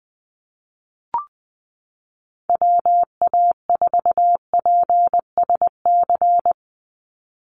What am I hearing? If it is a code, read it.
Morse code, heard as a keyed tone, about 20 wpm, 700 Hz. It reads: WA4PSC
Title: Call Signs Repeated 2x - 20wpm